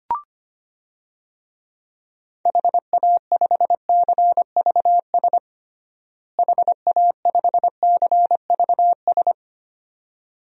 Morse code, heard as a keyed tone, about 25 wpm, 700 Hz. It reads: HA5CVS HA5CVS
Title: DXCC-100 Call Signs - 6 Characters - Repeated 2x - 25wpm